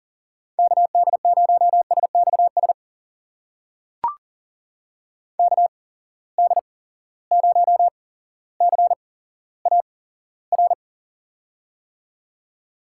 Morse code, heard as a keyed tone, about 40 wpm, spaced as for 17 wpm, 700 Hz. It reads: KD0SXS KD0CAR
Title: USA Call Signs - 6 Characters - 8x Character Spacing - 40wpm